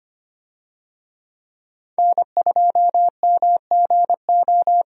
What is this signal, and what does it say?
Morse code, heard as a keyed tone, about 25 wpm, 690 Hz. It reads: N2MGO
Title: USA Call Signs - 5 Characters - 25wpm